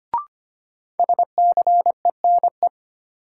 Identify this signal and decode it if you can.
Morse code, heard as a keyed tone, about 25 wpm, 700 Hz. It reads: SCENE
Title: Sentences from Top 1100 Words - New Word Drill - 25wpm